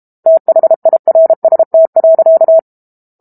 Morse code, heard as a keyed tone, about 32 wpm, 650 Hz. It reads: THIRST.
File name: Book Two / 06-Chapter 6 - 32 WPM